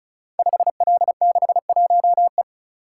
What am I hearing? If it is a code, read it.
Morse code, heard as a keyed tone, about 36 wpm, 700 Hz. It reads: 5L61E
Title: Code Groups - 5 Letter:Numbers - 35wpm